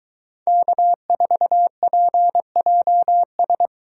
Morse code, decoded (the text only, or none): K4PJS